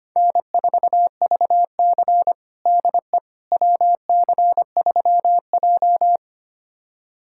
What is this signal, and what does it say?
Morse code, heard as a keyed tone, about 25 wpm, 700 Hz. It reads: N4VC DE WC3J
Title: USA Callsign DE Callsign - 4 Characters - 25wpm